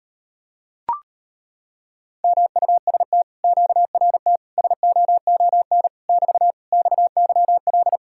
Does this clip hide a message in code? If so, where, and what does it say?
Yes: Morse code from 2.2 to 8.0 s, heard as a keyed tone, about 39 wpm, 700 Hz. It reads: MUST QRT SOON <BT> XYL